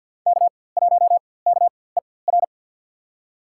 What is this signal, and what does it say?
Morse code, heard as a keyed tone, about 47 wpm, spaced as for 23 wpm, 700 Hz. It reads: K1KER
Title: USA Call Signs - 4x Character Spacing - 50wpm